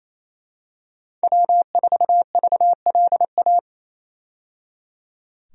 Morse code, heard as a keyed tone, about 28 wpm, 700 Hz. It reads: W4VLA
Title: USA Call Signs - 5 Characters - 28wpm